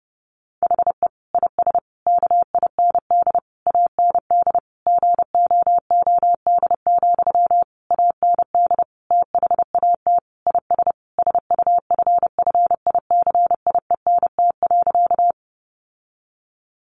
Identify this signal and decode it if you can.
Morse code, heard as a keyed tone, about 30 wpm, 700 Hz. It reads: HE IS KIND AND GOOD, AND THAT IS SUFFICIENT.
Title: Sentences from Top 1400 Words 30wpm